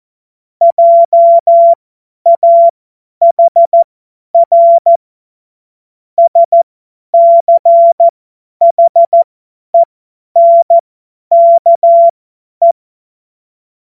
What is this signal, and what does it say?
Morse code, heard as a keyed tone, about 14 wpm, spaced as for 10 wpm, 680 Hz. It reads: JAHR SCHENKE
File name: Morsekurs / Witze / Witz 03. Januar 2023 50 F70